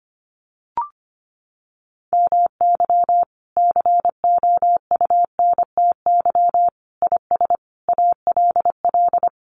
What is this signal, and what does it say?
Morse code, heard as a keyed tone, about 25 wpm, 700 Hz. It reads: MY COUNTY IS ALL
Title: Sentences from Top 1200 Words 25wpm